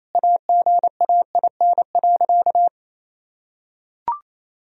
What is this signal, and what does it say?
Morse code, heard as a keyed tone, about 28 wpm, 700 Hz. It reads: AGAIN.
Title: Sentences from Top 1000 Words - Review Part 1 - 28wpm